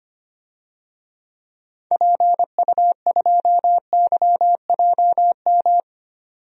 Morse code, heard as a keyed tone, about 25 wpm, 700 Hz. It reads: PU2YJM
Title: DXCC-100 Call Signs - 6 Characters - 25wpm